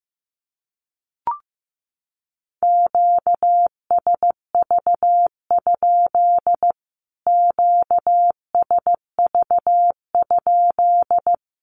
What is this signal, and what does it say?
Morse code, heard as a keyed tone, about 15 wpm, 700 Hz. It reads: QSV? QSV?
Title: Q-Codes Repeated 3x 15wpm